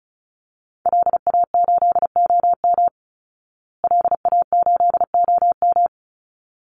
Morse code, heard as a keyed tone, about 34 wpm, 700 Hz. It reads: LA8OM LA8OM
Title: Call Signs Repeated 3x - 35wpm